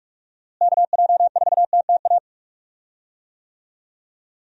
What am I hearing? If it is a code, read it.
Morse code, heard as a keyed tone, about 46 wpm, 700 Hz. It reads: KJ4TTA
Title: USA Call Signs - 6 Characters - Repeated 3x - 45wpm